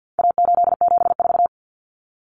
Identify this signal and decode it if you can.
Morse code, heard as a keyed tone, about 50 wpm, 700 Hz. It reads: A874
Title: Common Cut Numbers - 4-Digit 50wpm